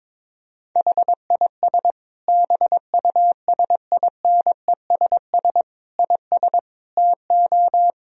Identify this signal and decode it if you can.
Morse code, heard as a keyed tone, about 22 wpm, 700 Hz. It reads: HIS BUSINESS IS TO
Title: Sentences from Top 1000 Words - Review Part 3 - 22wpm